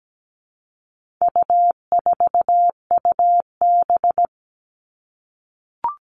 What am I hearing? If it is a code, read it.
Morse code, heard as a keyed tone, about 17 wpm, 700 Hz. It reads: U4UB
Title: All Cut Numbers - 4-Digit 17wpm